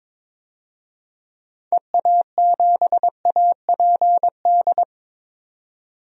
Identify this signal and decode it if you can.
Morse code, heard as a keyed tone, about 21 wpm, 700 Hz. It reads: EA7APD
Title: DXCC-100 Call Signs - 6 Characters - 22wpm